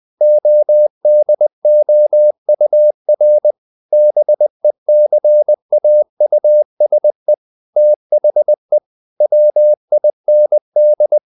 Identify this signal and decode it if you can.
Morse code, heard as a keyed tone, about 20 wpm, 600 Hz. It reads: ODOUR BECAUSE THE WIND
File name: the old man and the sea / the old man and the sea part 1 20wpm